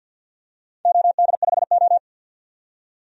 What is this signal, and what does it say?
Morse code, heard as a keyed tone, about 51 wpm, 700 Hz. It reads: OD5O
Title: DXCC-100 Call Signs - 4 Characters - 50wpm